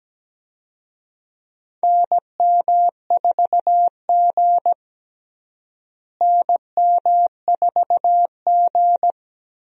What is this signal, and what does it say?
Morse code, heard as a keyed tone, about 17 wpm, 700 Hz. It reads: NM4G NM4G
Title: USA Call Signs - 4 Characters - Repeated 2x - 17wpm